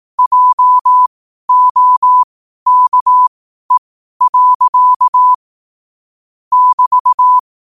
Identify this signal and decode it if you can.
Morse code, heard as a keyed tone, about 18 wpm, spaced as for 12 wpm, 1000 Hz. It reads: JOKE. <BT>
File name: 12wpm